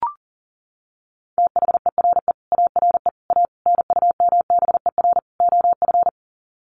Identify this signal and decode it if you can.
Morse code, heard as a keyed tone, about 40 wpm, 700 Hz. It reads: THERE ARE A NUMBER OF
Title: Sets of 5 Words 40wpm